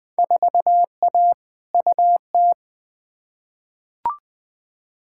Morse code, heard as a keyed tone, about 21 wpm, 700 Hz. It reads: 4A UT
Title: Field Day Exchanges - 20wpm